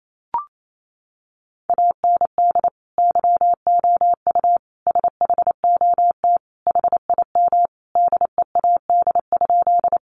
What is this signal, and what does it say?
Morse code, heard as a keyed tone, about 28 wpm, 700 Hz. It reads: AND YOU SHOT HIM DEAD?
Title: Sentences from Top 1500 Words 28wpm